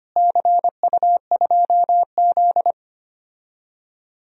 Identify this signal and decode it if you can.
Morse code, heard as a keyed tone, about 25 wpm, 700 Hz. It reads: CU2Z
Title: DXCC-100 Call Signs - 4 Characters - Repeated 2x - 25wpm